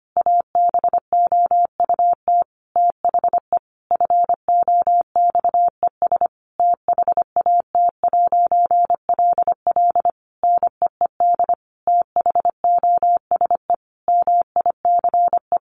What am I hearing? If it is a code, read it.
Morse code, heard as a keyed tone, about 25 wpm, 700 Hz. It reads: ABOUT THE FOXES THAT'LL NEED THOSE MICE
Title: A Sound Of Thunder 25wpm